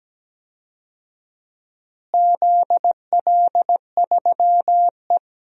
Morse code, heard as a keyed tone, about 17 wpm, 700 Hz. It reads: ZL3E
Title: DXCC-100 Call Signs - 4 Characters - Repeated 3x - 17wpm